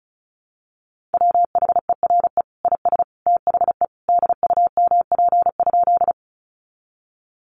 Morse code, heard as a keyed tone, about 34 wpm, 700 Hz. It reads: WHERE IS THE DUMP?
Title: Sentences from Top 1300 Words 35wpm